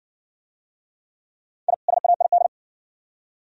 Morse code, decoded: I SAID

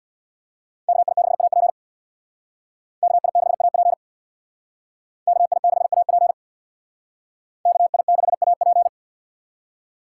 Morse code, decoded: KI6AP KI6AP KI6AP KI6AP

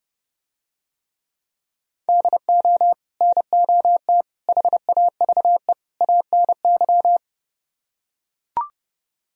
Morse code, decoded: DO NOT HAVE ANY

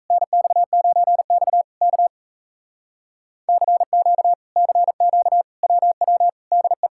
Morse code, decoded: NX9X K CQ CQ WW DE

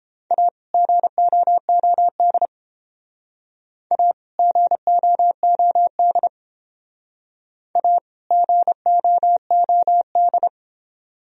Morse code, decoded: A GOOD A GOOD A GOOD

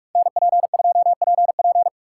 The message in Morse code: NP2PP

